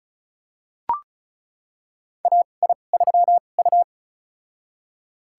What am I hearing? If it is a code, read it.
Morse code, heard as a keyed tone, about 34 wpm, spaced as for 25 wpm, 700 Hz. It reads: AI3U
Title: USA Call Signs - 4 Characters - 2x Character Spacing - 35wpm